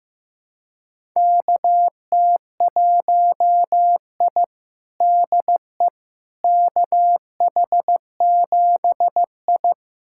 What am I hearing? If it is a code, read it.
Morse code, heard as a keyed tone, about 15 wpm, 700 Hz. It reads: KT1I DE KH7I
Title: USA Callsign DE Callsign - 4 Characters - 15wpm